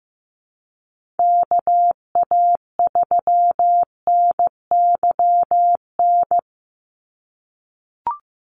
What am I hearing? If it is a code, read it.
Morse code, heard as a keyed tone, about 15 wpm, 700 Hz. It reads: KA3NYN
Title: USA Call Signs 15wpm